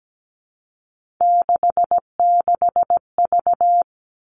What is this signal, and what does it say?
Morse code, heard as a keyed tone, about 17 wpm, 700 Hz. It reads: 66V
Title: Three Challenging Letter-Number Combinations 17wpm